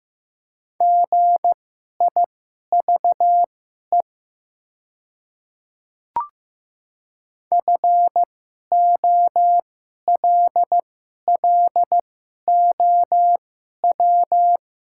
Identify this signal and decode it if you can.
Morse code, heard as a keyed tone, about 15 wpm, spaced as for 11 wpm, 700 Hz. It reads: GIVE FOLLOW
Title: 2x Character Spacing - Top 200 Words - 15wpm